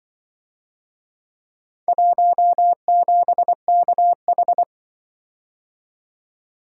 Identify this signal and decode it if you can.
Morse code, heard as a keyed tone, about 24 wpm, 700 Hz. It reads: 17KH
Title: Speed-Racing - Four Letter-Number Combinations - 20wpm